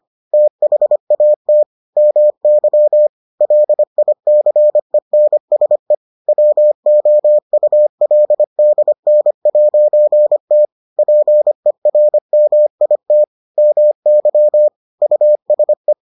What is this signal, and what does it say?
Morse code, heard as a keyed tone, about 25 wpm, 600 Hz. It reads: THAT MY LICENSE WOULDN'T PERMIT MY USE